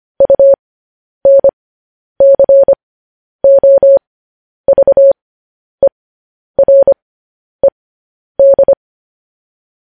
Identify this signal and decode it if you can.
Morse code, heard as a keyed tone, about 25 wpm, spaced as for 10 wpm, 550 Hz. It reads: UNCOVERED
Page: News Headlines in Morse Code at 10 WPM